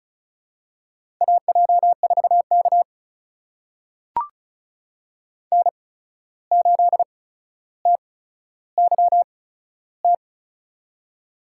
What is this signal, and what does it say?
Morse code, heard as a keyed tone, about 36 wpm, spaced as for 14 wpm, 700 Hz. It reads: AJ4K N8TYT
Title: USA Call Signs - 8x Character Spacing - 35wpm